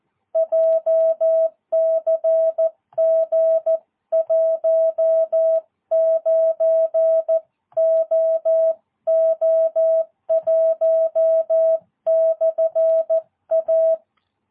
Morse code, heard as a keyed tone, about 14 wpm, 650 Hz. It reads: JCG 19OO1/A
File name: jh8cbh / 2022cw